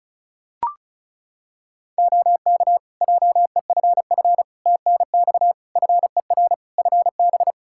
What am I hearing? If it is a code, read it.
Morse code, heard as a keyed tone, about 34 wpm, 700 Hz. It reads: OK JEFF TNX FER FB
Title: Practice QSO - Exchange 4 - Ending - 35wpm